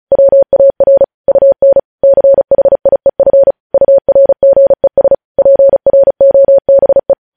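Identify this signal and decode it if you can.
Morse code, heard as a keyed tone, about 36 wpm, 550 Hz. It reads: WAR UN CHIEF URGES PROBE
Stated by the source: Morse code practice sending news headlines at 35 wpm.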